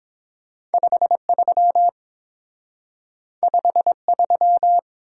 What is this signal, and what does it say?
Morse code, heard as a keyed tone, about 30 wpm, 700 Hz. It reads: 53 53